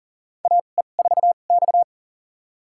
Morse code, heard as a keyed tone, about 40 wpm, spaced as for 29 wpm, 700 Hz. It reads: AE4X
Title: USA Call Signs - 4 Characters - 2x Character Spacing - 40wpm